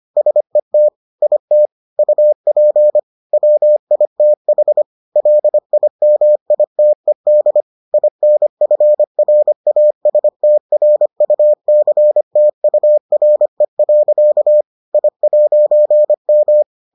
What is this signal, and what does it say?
Morse code, heard as a keyed tone, about 25 wpm, 600 Hz. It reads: SET IT UP WITH LIMITED INFRASTRUCTURE. I'M